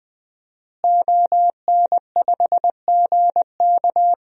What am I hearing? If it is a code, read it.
Morse code, heard as a keyed tone, about 20 wpm, 700 Hz. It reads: ON5GK
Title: DXCC-100 Call Signs - 5 Characters - Repeated 3x - 20wpm